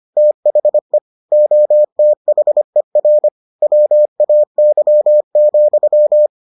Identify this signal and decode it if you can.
Morse code, heard as a keyed tone, about 25 wpm, 600 Hz. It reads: THE OTHER WAY,